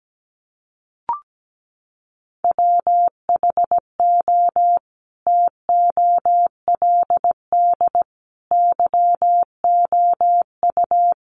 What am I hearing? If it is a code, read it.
Morse code, heard as a keyed tone, about 17 wpm, 700 Hz. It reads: WHO TOLD YOU
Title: Sentences from Top 400 Words 17wpm